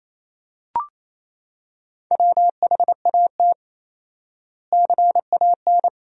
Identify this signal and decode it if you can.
Morse code, heard as a keyed tone, about 28 wpm, 700 Hz. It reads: WHAT CAN
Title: 4x Word-Spacing - Sentences from Top 400 Words - 28wpm